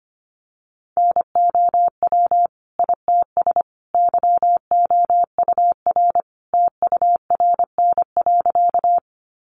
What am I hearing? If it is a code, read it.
Morse code, heard as a keyed tone, about 25 wpm, 700 Hz. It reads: NOW ITS YOUR TURN.
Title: Sentences from Top 600 Words 25wpm